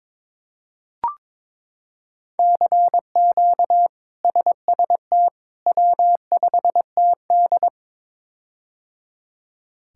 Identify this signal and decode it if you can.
Morse code, heard as a keyed tone, about 22 wpm, 700 Hz. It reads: CQ SST W5TD
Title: K1USN SST QSO Practice - 22wpm